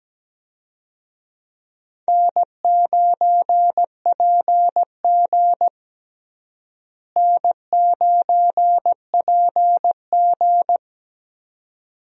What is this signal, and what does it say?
Morse code, heard as a keyed tone, about 17 wpm, 700 Hz. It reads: N9PG N9PG
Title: USA Call Signs - 4 Characters - Repeated 2x - 17wpm